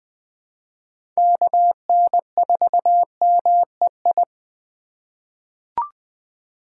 Morse code, heard as a keyed tone, about 20 wpm, 700 Hz. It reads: KN4MEI
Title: USA Call Signs - 6 Characters - 4x Character Spacing - 20wpm